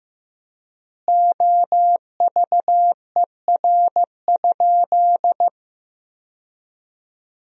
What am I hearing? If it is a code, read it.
Morse code, heard as a keyed tone, about 15 wpm, 700 Hz. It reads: OVER?
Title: 4x Word-Spacing - Sentences from Top 100 Words - 15wpm